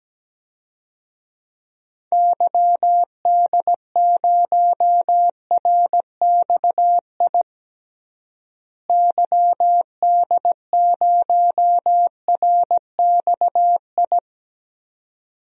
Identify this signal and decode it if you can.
Morse code, heard as a keyed tone, about 17 wpm, 700 Hz. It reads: YD0RXI YD0RXI
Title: DXCC-100 Call Signs - 6 Characters - Repeated 2x - 17wpm